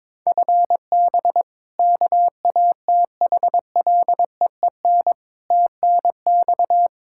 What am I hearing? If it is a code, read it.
Morse code, heard as a keyed tone, about 21 wpm, 700 Hz. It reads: FB KATHLEEN TNX
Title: Practice QSO - Exchange 3 - More Info - 22wpm